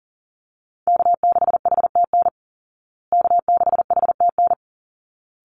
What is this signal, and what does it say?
Morse code, heard as a keyed tone, about 40 wpm, 700 Hz. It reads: K6HTN K6HTN
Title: Call Signs Repeated 3x - 40wpm